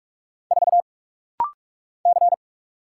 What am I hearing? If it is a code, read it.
Morse code, heard as a keyed tone, about 44 wpm, 700 Hz. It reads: 4 C
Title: Single Letter-Number Alt - Rapid-Fire 45wpm